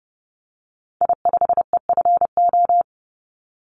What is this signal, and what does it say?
Morse code, heard as a keyed tone, about 30 wpm, 700 Hz. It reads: I5EFO
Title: Call Signs - 30wpm